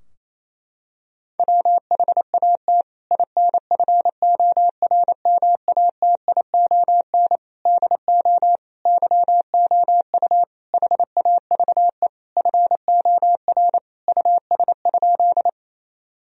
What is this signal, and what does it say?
Morse code, heard as a keyed tone, about 28 wpm, 700 Hz. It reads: WHAT INFORMATION DO YOU HAVE FOR US?
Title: Sentences from Top 1000 Words - Review Part 3 - 28wpm